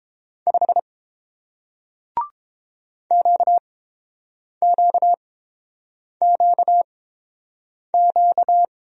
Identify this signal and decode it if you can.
Morse code, heard as a keyed tone, about 30 wpm, spaced as for 7 wpm, 700 Hz. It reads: 5 QQQQ